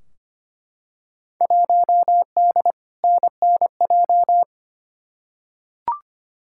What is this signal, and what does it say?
Morse code, heard as a keyed tone, about 25 wpm, 700 Hz. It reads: 1D NNJ
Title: Field Day Exchanges - 25wpm